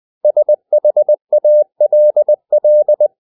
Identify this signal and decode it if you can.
Morse code, heard as a keyed tone, about 20 wpm, 600 Hz. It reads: SHALL
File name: cw / 100 Common English Words